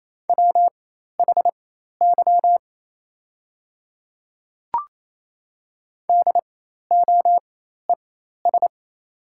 Morse code, heard as a keyed tone, about 28 wpm, spaced as for 13 wpm, 700 Hz. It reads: WHY DOES
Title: 4x Character Spacing - Top 500 Words - 28wpm